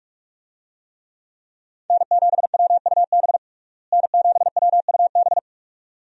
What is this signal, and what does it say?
Morse code, heard as a keyed tone, about 46 wpm, 700 Hz. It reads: N7WUB N7WUB
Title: USA Call Signs - 5 Characters - Repeated 2x - 45wpm